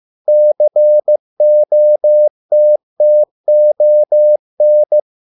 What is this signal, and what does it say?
Morse code, heard as a keyed tone, about 15 wpm, 600 Hz. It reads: COTTON